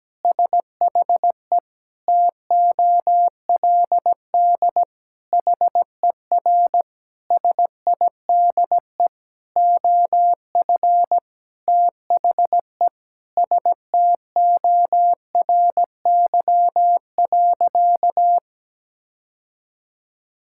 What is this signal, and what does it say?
Morse code, heard as a keyed tone, about 17 wpm, 700 Hz. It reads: SHE TOLD HER SIDE OF THE STORY.
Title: Sentences from Top 1000 Words - Review Part 6 - 17wpm